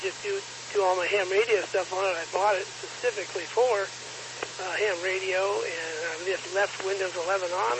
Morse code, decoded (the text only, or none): none